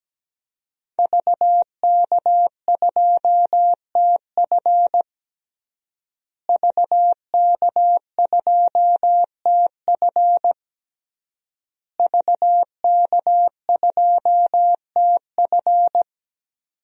Morse code, heard as a keyed tone, about 17 wpm, 700 Hz. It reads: VK2TF VK2TF VK2TF
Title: DXCC-100 Call Signs - 5 Characters - Repeated 3x - 17wpm